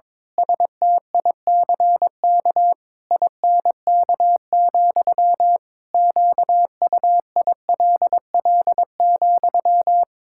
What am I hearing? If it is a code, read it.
Morse code, heard as a keyed tone, about 22 wpm, 700 Hz. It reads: STICK INK, QUILL,
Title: Pangram Challenge v2 - 22wpm